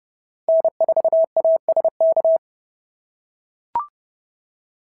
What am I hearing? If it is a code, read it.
Morse code, heard as a keyed tone, about 30 wpm, 690 Hz. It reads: N4ASK
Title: SOTA QSO Practice 30wpm